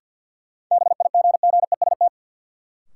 Morse code, heard as a keyed tone, about 48 wpm, 700 Hz. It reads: BIGGEST
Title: Sets of 4 Words - New Word Drill - 50wpm